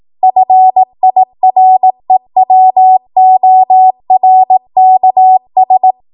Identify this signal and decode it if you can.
Morse code, heard as a keyed tone, about 18 wpm, 750 Hz. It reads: FIREWORKS
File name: Archive / 18 WPM / 18WPM